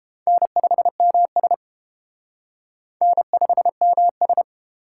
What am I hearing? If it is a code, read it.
Morse code, heard as a keyed tone, about 30 wpm, 700 Hz. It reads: N5MS N5MS